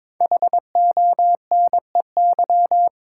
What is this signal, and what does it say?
Morse code, heard as a keyed tone, about 22 wpm, 700 Hz. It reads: HONEY
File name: 5 Letter Words - First 3 Letters Given - 22wpm